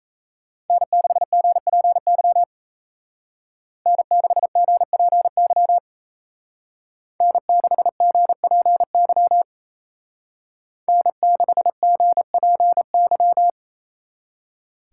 Morse code, heard as a keyed tone, about 32 wpm, 700 Hz. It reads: N6GPY N6GPY N6GPY N6GPY
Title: Speed Racing - USA 1x3 Call Signs 28wpm